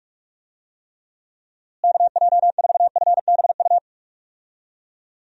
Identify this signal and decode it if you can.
Morse code, heard as a keyed tone, about 46 wpm, 700 Hz. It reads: KJ4FBU
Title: USA Call Signs - 6 Characters - Repeated 2x - 45wpm